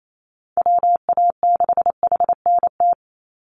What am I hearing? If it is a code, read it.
Morse code, heard as a keyed tone, about 28 wpm, 700 Hz. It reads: WA6HNT
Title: USA Call Signs 28wpm